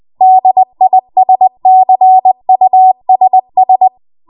Morse code, heard as a keyed tone, about 20 wpm, 750 Hz. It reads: DISCUSS
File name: Morse / 20WPM